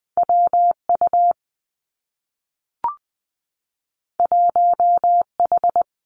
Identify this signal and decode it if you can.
Morse code, heard as a keyed tone, about 20 wpm, 700 Hz. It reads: W U 1 H
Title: Two Challenging Letter-Number Combinations 20wpm